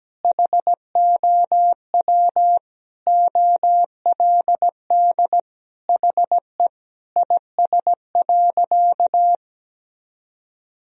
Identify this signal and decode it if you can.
Morse code, heard as a keyed tone, about 17 wpm, 700 Hz. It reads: HOW OLD HE IS.
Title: Sentences from Top 1000 Words - Review Part 4 - 17wpm